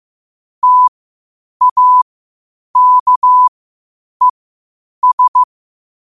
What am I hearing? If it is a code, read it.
Morse code, heard as a keyed tone, about 15 wpm, spaced as for 8 wpm, 1000 Hz. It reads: TAKES
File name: qotdfiles / 8wpm